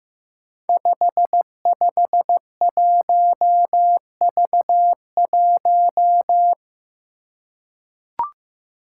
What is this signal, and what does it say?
Morse code, heard as a keyed tone, about 15 wpm, 700 Hz. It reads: 551V1
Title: Five Challenging Letter-Number Combinations 15wpm